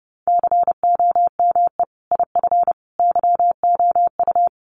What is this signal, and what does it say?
Morse code, heard as a keyed tone, about 30 wpm, 700 Hz. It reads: COME IF YOU
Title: Sentences from Top 1000 Words 30wpm